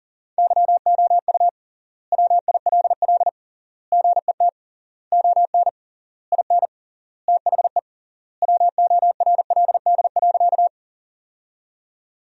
Morse code, heard as a keyed tone, about 37 wpm, 700 Hz. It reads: YOU WILL GET ON IN THE WORLD.
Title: 3x Word-Spacing - Sentences from Top 200 Words - 40wpm